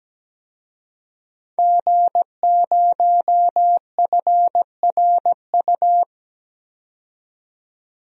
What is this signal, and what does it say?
Morse code, heard as a keyed tone, about 17 wpm, 700 Hz. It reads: G0FRU
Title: DXCC-100 Call Signs - 5 Characters - 17wpm